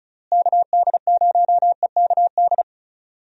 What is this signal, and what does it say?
Morse code, heard as a keyed tone, about 34 wpm, 700 Hz. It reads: KD0EKD
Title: USA Call Signs - 6 Characters - Repeated 3x - 35wpm